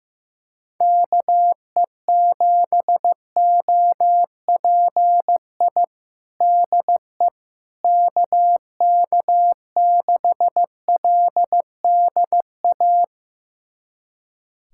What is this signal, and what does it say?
Morse code, heard as a keyed tone, about 15 wpm, 700 Hz. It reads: KE7OPI DE KK6LDA
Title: USA Callsign DE Callsign - 6 Characters - 15wpm